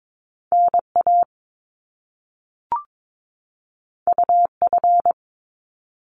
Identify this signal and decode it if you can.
Morse code, heard as a keyed tone, about 22 wpm, 700 Hz. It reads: NA UF
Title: Top Two Letter Combinations in Words 22wpm